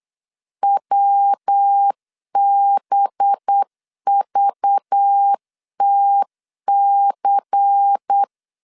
Morse code, heard as a keyed tone, about 8 wpm, 800 Hz. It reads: WBVTC
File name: LEZIONI / 7 GF9Z / 7 G 8wpm 3118